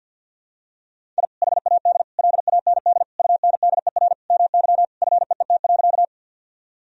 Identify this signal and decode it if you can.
Morse code, heard as a keyed tone, about 52 wpm, 700 Hz. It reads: I HAD LAND UNDER MY FEET.
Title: Sentences from Top 1000 Words - Review Part 1 - 50wpm